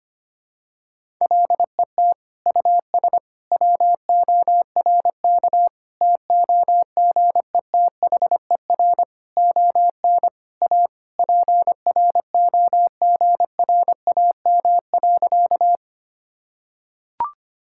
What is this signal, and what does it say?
Morse code, heard as a keyed tone, about 25 wpm, 700 Hz. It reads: LET US WORK TOGETHER ON A PROGRAM.
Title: Sentences from Top 1000 Words - Review Part 5 - 25wpm